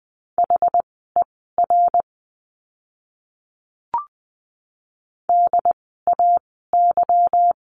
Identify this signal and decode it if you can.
Morse code, heard as a keyed tone, about 20 wpm, spaced as for 15 wpm, 700 Hz. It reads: HER DAY
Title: 192 Lesson 31f - Letter v - Words 20wpm